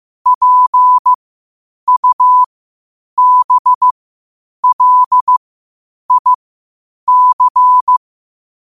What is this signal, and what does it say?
Morse code, heard as a keyed tone, about 15 wpm, spaced as for 8 wpm, 1000 Hz. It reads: PUBLIC
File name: qotdfiles / 8wpm